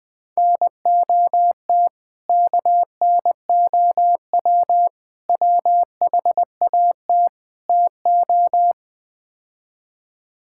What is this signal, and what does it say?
Morse code, heard as a keyed tone, about 21 wpm, 700 Hz. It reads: NOT KNOW WHAT TO
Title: Build A Phrase - 4-Word Phrases - 20wpm